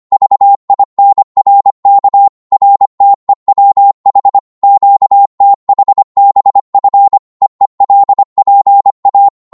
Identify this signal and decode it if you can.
Morse code, heard as a keyed tone, about 25 wpm, 800 Hz. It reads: VINRK RTEWH QTHBF EELPA
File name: Morse / ltr-25wpm